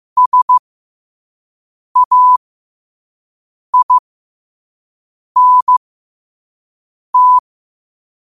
Morse code, heard as a keyed tone, about 15 wpm, spaced as for 5 wpm, 1000 Hz. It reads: SAINT
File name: qotdfiles / 5wpm